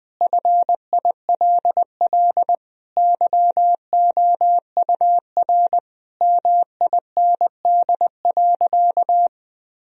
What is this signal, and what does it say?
Morse code, heard as a keyed tone, about 20 wpm, 700 Hz. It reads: FILL YOUR MIND.
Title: Sentences from Top 1000 Words - Review Part 3 - 20wpm